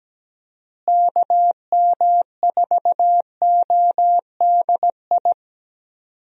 Morse code, heard as a keyed tone, about 17 wpm, 700 Hz. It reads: KM4ODI
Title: USA Call Signs - 6 Characters - Repeated 2x - 17wpm